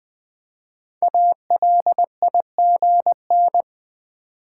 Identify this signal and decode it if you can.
Morse code, heard as a keyed tone, about 20 wpm, 700 Hz. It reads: ALIGN
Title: 5 Letter Words - First 3 Letters Given - 20wpm